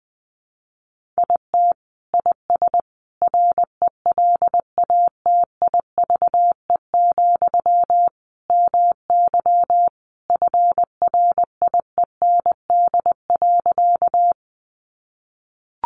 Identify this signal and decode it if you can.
Morse code, heard as a keyed tone, about 20 wpm, 700 Hz. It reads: IT IS RELATIVE, MY FRIEND.
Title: Sentences from Top 1400 Words 20wpm